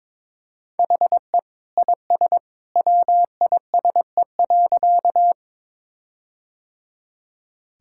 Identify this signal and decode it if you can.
Morse code, heard as a keyed tone, about 22 wpm, 700 Hz. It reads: HE IS WISE.
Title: Sentences from Top 1600 Words 22wpm